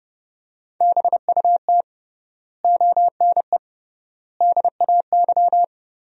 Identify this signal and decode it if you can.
Morse code, heard as a keyed tone, about 32 wpm, 700 Hz. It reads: BUT ONE DAY
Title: 3x Word-Spacing - Sentences from Top 400 Words - 30wpm